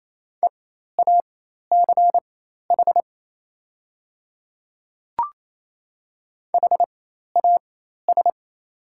Morse code, heard as a keyed tone, about 28 wpm, spaced as for 13 wpm, 700 Hz. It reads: EACH HAS
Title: 4x Character Spacing - Top 100 Words - 28wpm